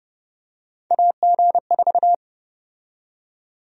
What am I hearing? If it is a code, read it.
Morse code, heard as a keyed tone, about 30 wpm, 700 Hz. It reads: AG4
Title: USA Call Sign Prefixes - 30wpm